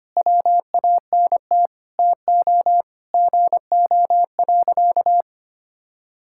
Morse code, heard as a keyed tone, about 25 wpm, 700 Hz. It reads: WANT TO GO.
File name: practice / 25wpm / Sentences with 4-Word Phrases - 25wpm